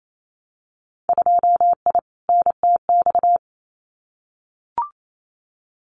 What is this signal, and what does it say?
Morse code, heard as a keyed tone, about 28 wpm, 700 Hz. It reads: 2I NTX
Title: Winter Field Day Exchanges Encore 28wpm